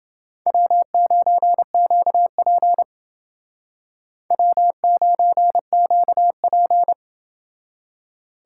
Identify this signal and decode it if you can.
Morse code, heard as a keyed tone, about 29 wpm, 700 Hz. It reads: W9QP W9QP